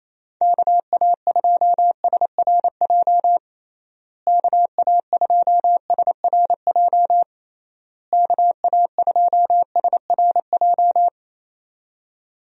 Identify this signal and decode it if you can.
Morse code, heard as a keyed tone, about 28 wpm, 700 Hz. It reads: KA2SRJ KA2SRJ KA2SRJ
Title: USA Call Signs - 6 Characters - Repeated 3x - 28wpm